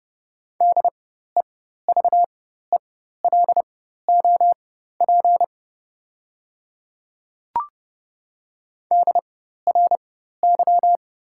Morse code, heard as a keyed tone, about 30 wpm, spaced as for 14 wpm, 700 Hz. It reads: DEVELOP DRY
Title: 4x Character Spacing - Top 500 Words - 30wpm